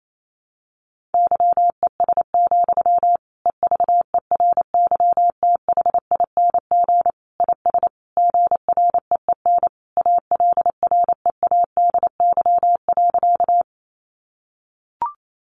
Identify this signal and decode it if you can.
Morse code, heard as a keyed tone, about 28 wpm, 700 Hz. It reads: YES, EVERYTHING IS GREEN ALREADY.
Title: Sentences from Top 700 Words 28wpm